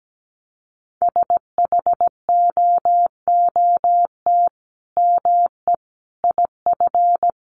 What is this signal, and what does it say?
Morse code, heard as a keyed tone, about 17 wpm, 700 Hz. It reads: SHOOT ME IF
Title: Sentences from Top 1200 Words 17wpm